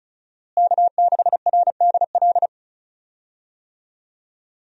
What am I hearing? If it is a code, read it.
Morse code, heard as a keyed tone, about 36 wpm, 700 Hz. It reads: K6RDL
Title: USA Call Signs - 5 Characters - Repeated 3x - 35wpm